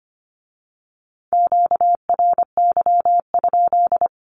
Morse code, heard as a keyed tone, about 25 wpm, 700 Hz. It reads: QRY?
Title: Q-Codes with Spoken Meaning - 25wpm